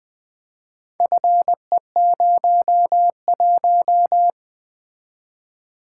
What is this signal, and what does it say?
Morse code, heard as a keyed tone, about 20 wpm, 700 Hz. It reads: FE01
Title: Four Letter-Number Combinations - 20wpm